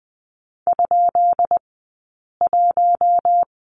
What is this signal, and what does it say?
Morse code, heard as a keyed tone, about 20 wpm, spaced as for 8 wpm, 700 Hz. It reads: ?1
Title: 171 Lesson 28d - Number 9 - Two Characters 20wpm